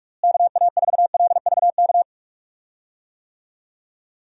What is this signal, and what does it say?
Morse code, heard as a keyed tone, about 43 wpm, 700 Hz. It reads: KA4LVK
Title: USA Call Signs - 6 Characters - Repeated 3x - 45wpm